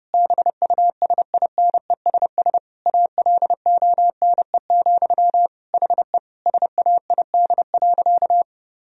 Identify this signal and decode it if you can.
Morse code, heard as a keyed tone, about 30 wpm, 700 Hz. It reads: BUSINESS ALONE, HE SAID.
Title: Sentences from Top 1000 Words - Review Part 2 - 30wpm